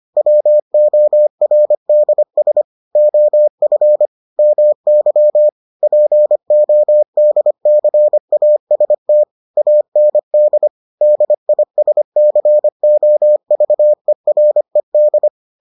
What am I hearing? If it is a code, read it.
Morse code, heard as a keyed tone, about 25 wpm, 600 Hz. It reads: WORDS OF MY PODCAST AND DISCOVERED